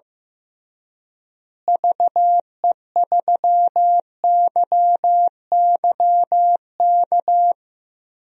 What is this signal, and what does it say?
Morse code, heard as a keyed tone, about 15 wpm, 700 Hz. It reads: VE3YYK
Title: DXCC-100 Call Signs - 6 Characters - Repeated 3x - 15wpm